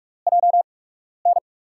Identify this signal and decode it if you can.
Morse code, heard as a keyed tone, about 47 wpm, spaced as for 12 wpm, 700 Hz. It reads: JN
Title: USA Call Signs - 5 Characters - 8x Character Spacing - 45wpm